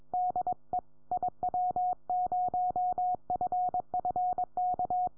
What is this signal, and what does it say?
Morse code, heard as a keyed tone, about 22 wpm, 730 Hz. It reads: DE IW0FFK
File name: iw0ffk / 2008 06 28 CY0X IW0FFK